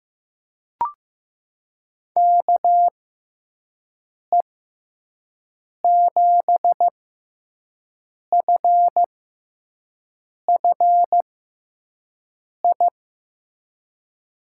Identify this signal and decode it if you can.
Morse code, heard as a keyed tone, about 15 wpm, spaced as for 5 wpm, 700 Hz. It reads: KE7FFI
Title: USA Call Signs - 6x Character Spacing - 15wpm